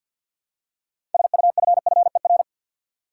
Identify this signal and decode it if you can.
Morse code, heard as a keyed tone, about 52 wpm, 700 Hz. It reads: SUFFER
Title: Six Letter Words - 50wpm